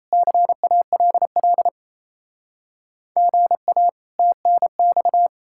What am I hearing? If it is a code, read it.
Morse code, heard as a keyed tone, about 36 wpm, 700 Hz. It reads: CALL GA TNX